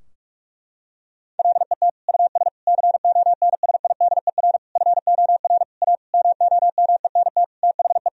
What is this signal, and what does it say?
Morse code, heard as a keyed tone, about 46 wpm, 700 Hz. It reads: LET US CONSIDER FOR A MOMENT THE
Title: Sentences from Top 1000 Words - Review Part 4 - 45wpm